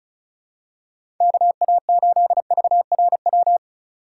Morse code, heard as a keyed tone, about 36 wpm, 700 Hz. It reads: KA8VRW
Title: USA Call Signs - 2x Character Spacing - 35wpm